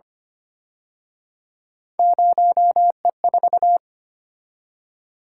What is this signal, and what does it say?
Morse code, heard as a keyed tone, about 29 wpm, 700 Hz. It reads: 0E4
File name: Speed-Racing - Three Letter-Number Combinations - 25wpm